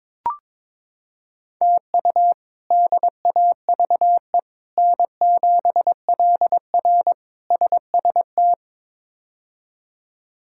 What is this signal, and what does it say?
Morse code, heard as a keyed tone, about 22 wpm, 700 Hz. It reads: TU DAVE N7LR SST
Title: K1USN SST QSO Practice - 22wpm